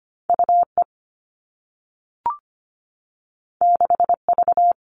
Morse code, heard as a keyed tone, about 25 wpm, 700 Hz. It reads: UE 6V
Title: All Cut Numbers - 2-Digit 25wpm